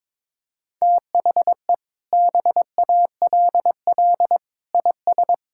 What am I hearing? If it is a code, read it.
Morse code, heard as a keyed tone, about 22 wpm, 700 Hz. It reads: THE BALL IS
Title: Sentences from Top 1000 Words - Review Part 6 - 22wpm